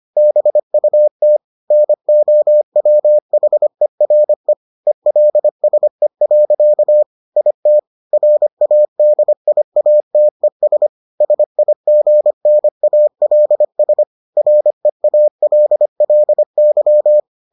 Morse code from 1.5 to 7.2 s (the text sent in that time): NOWHERE ELSE.